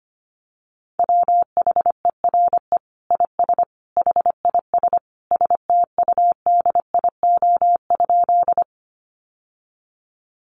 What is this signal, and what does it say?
Morse code, heard as a keyed tone, about 25 wpm, 700 Hz. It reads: WHERE IS HIS STUDIO?
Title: Sentences from Top 1200 Words 25wpm